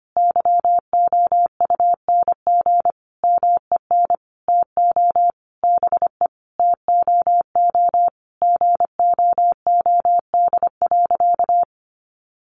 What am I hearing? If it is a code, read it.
Morse code, heard as a keyed tone, about 25 wpm, 700 Hz. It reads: YOUNG MEN TO BE TOO GOOD.
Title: Sentences from Top 1200 Words 25wpm